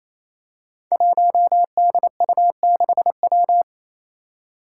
Morse code, heard as a keyed tone, about 28 wpm, 700 Hz. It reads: 1DU6W
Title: Five Challenging Letter-Number Combinations 28wpm